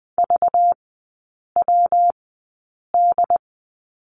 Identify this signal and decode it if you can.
Morse code, heard as a keyed tone, about 20 wpm, spaced as for 8 wpm, 700 Hz. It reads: VWD
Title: 205 Lesson 33e - Letter j - Three Characters 20wpm